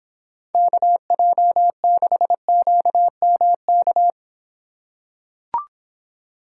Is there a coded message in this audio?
Yes: Morse code from 0.6 to 4.1 s, heard as a keyed tone, about 25 wpm, 700 Hz. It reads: KJ6QMK